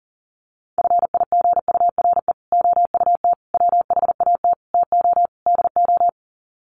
Morse code, heard as a keyed tone, about 40 wpm, 700 Hz. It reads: FIGURE OUT WHAT TO DO
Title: Sets of 5 Words Encore2 40wpm